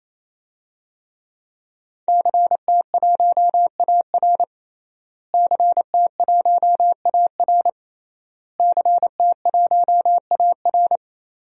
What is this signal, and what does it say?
Morse code, heard as a keyed tone, about 28 wpm, 700 Hz. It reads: CT1AR CT1AR CT1AR
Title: DXCC-100 Call Signs - 5 Characters - Repeated 3x - 28wpm